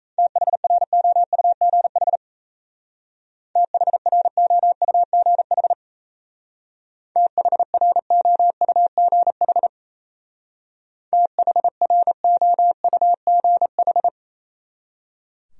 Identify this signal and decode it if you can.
Morse code, heard as a keyed tone, about 36 wpm, 700 Hz. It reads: THROUGH THROUGH THROUGH THROUGH